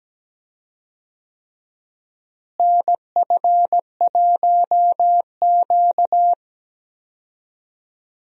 Morse code, heard as a keyed tone, about 17 wpm, 700 Hz. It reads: NF1Q
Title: USA Call Signs - 4 Characters - Repeated 3x - 17wpm